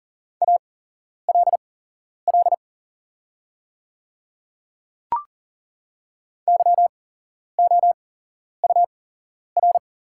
Morse code, heard as a keyed tone, about 40 wpm, spaced as for 11 wpm, 700 Hz. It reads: ALL YOUR
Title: 8x Character Spacing - Top 100 Words - 40wpm